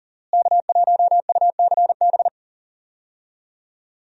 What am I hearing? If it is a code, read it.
Morse code, heard as a keyed tone, about 37 wpm, 700 Hz. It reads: K1UCB
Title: USA Call Signs - 5 Characters - Repeated 2x - 40wpm